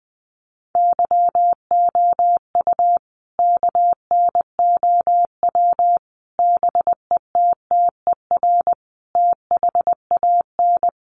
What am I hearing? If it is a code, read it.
Morse code, heard as a keyed tone, about 20 wpm, 700 Hz. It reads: YOU KNOW BETTER THAN
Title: Sentences from Top 400 Words 20wpm